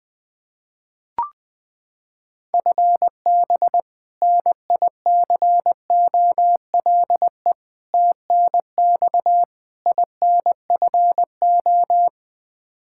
Morse code, heard as a keyed tone, about 20 wpm, 700 Hz. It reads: FB NICOLE TNX INFO
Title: Practice QSO - Exchange 3 - More Info - 20wpm